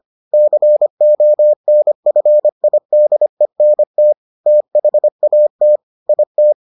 Morse code, heard as a keyed tone, about 25 wpm, 600 Hz. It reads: CONFIDENT THAT IT